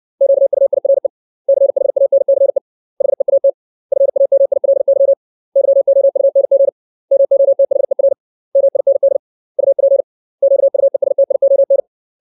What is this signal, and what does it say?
Morse code, heard as a keyed tone, about 60 wpm, 550 Hz. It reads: QUIRE CHANCE HEAT FAMILY YOUNG MOTHER MIND UP QUESTION
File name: 632 words / 632 Words 60WPM